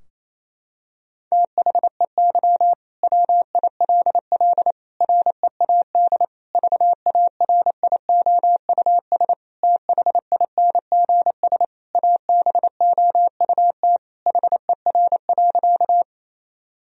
Morse code, heard as a keyed tone, about 28 wpm, 700 Hz. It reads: THEY WILL READ VARIOUS THINGS ABOUT HER.
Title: Sentences from Top 1000 Words - Review Part 4 - 28wpm